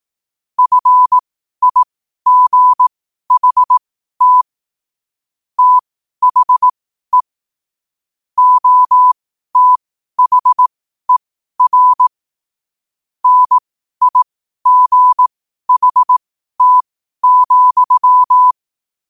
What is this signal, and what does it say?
Morse code, heard as a keyed tone, about 18 wpm, spaced as for 12 wpm, 1000 Hz. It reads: FIGHT THE OTHER NIGHT,
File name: qotdfiles / 12wpm